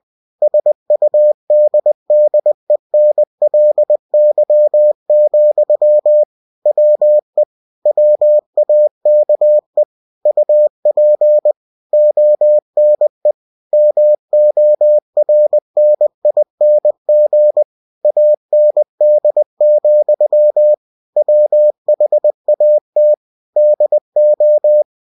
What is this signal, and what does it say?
Morse code, heard as a keyed tone, about 20 wpm, 600 Hz. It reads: SUDDENLY, WE WAKE UP ONE MORNING AND, WHAT DO